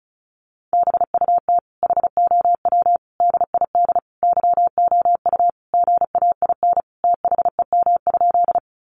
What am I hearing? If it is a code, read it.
Morse code, heard as a keyed tone, about 34 wpm, 700 Hz. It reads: BUT HOW DID YOU GAIN THEM?
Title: Sentences from Top 1000 Words 35wpm